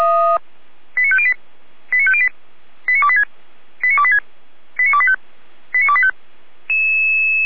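Five-tone selective call (ZVEI), heard as quick runs of five tones, tones of about 70 ms. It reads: 89508 89508 89286 89286 89285 89285
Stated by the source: ZVEI five-tone selective call received by SDR.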